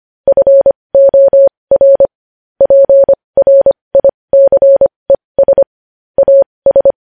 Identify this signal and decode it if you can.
Morse code, heard as a keyed tone, about 25 wpm, spaced as for 20 wpm, 550 Hz. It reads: FOR PRICES AS
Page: news headlines in morse code at 20 wpm